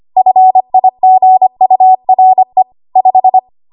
Morse code, heard as a keyed tone, about 25 wpm, 750 Hz. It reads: FIGURE 5
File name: Archive / 25 WPM / 25WPM